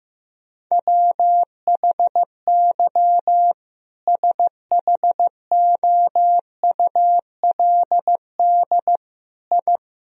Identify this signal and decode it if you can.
Morse code, heard as a keyed tone, about 16 wpm, 700 Hz. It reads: WHY SHOULD I
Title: Sentences from Top 1000 Words - Review Part 3 - 15wpm